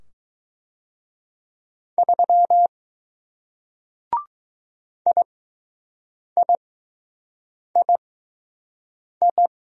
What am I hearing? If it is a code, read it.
Morse code, heard as a keyed tone, about 20 wpm, spaced as for 6 wpm, 700 Hz. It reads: 3 IIII